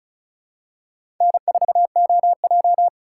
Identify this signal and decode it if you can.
Morse code, heard as a keyed tone, about 36 wpm, 700 Hz. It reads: N4OJ
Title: USA Call Signs - 3x Character Spacing - 35wpm